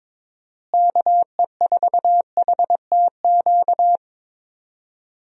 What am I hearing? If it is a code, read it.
Morse code, heard as a keyed tone, about 22 wpm, 700 Hz. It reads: KE4HTQ
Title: USA Call Signs - 6 Characters - Repeated 2x - 22wpm